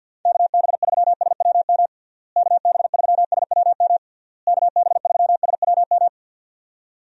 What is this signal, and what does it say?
Morse code, heard as a keyed tone, about 51 wpm, 700 Hz. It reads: KB3SWM KB3SWM KB3SWM
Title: USA Call Signs - 6 Characters - Repeated 3x - 50wpm